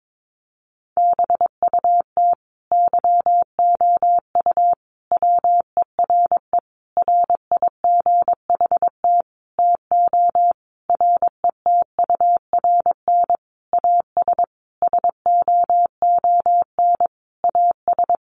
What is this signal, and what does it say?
Morse code, heard as a keyed tone, about 22 wpm, 700 Hz. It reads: BUT YOU WERE RIGHT TO RETURN AS SOON AS
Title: Sentences from Top 800 Words 22wpm